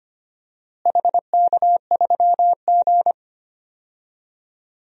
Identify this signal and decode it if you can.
Morse code, heard as a keyed tone, about 25 wpm, 700 Hz. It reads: HK3G
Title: DXCC-100 Call Signs - 4 Characters - Repeated 3x - 25wpm